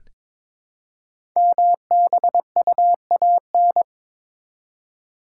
Morse code, heard as a keyed tone, about 23 wpm, 700 Hz. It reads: MBUAN
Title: Code Groups - 5 Letters - 22wpm